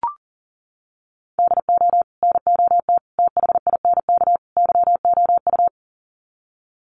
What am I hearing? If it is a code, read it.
Morse code, heard as a keyed tone, about 40 wpm, 700 Hz. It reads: DO NOT THINK YOU
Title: Sets of 4 Words 40wpm